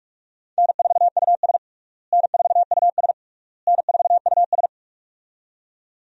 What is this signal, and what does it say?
Morse code, heard as a keyed tone, about 46 wpm, 700 Hz. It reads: N4US N4US N4US
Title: USA Call Signs - 4 Characters - Repeated 3x - 45wpm